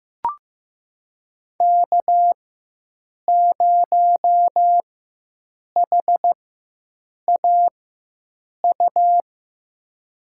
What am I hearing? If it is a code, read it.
Morse code, heard as a keyed tone, about 15 wpm, spaced as for 7 wpm, 700 Hz. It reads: K0HAU
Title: USA Call Signs - 5 Characters - 4x Character Spacing - 15wpm